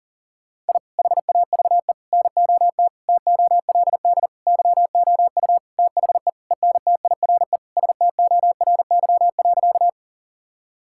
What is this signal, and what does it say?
Morse code, heard as a keyed tone, about 40 wpm, 700 Hz. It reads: I HAVE NOT TOLD YOU THE ENTIRE STORY.
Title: Sentences from Top 1000 Words - Review Part 5 - 40wpm